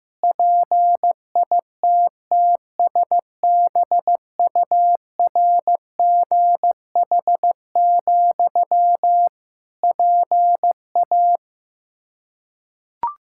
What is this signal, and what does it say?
Morse code, heard as a keyed tone, about 15 wpm, 700 Hz. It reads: PITTSBURGH, PA
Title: Practice QSO - Exchange 1 - Beginning - 15wpm